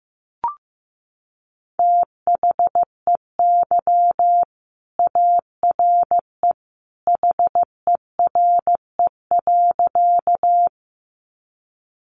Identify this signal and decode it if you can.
Morse code, heard as a keyed tone, about 15 wpm, 700 Hz. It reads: THEY ARE HERE.
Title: Sentences from Top 200 Words 15wpm